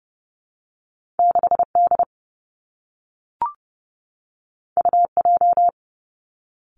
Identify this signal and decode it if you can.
Morse code, heard as a keyed tone, about 30 wpm, 700 Hz. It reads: 6D UJ
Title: Two Challenging Letter-Number Combinations 30wpm